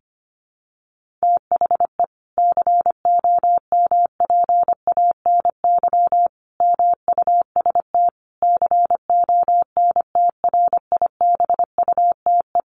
Morse code, heard as a keyed tone, about 25 wpm, 700 Hz. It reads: THE COMPANY MUST CONTRIBUTE
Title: Sentences from Top 1400 Words 25wpm